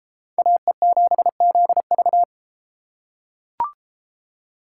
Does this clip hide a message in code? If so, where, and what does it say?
Yes: Morse code from 0.4 to 2.2 s, heard as a keyed tone, about 31 wpm, 700 Hz. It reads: AE7ZV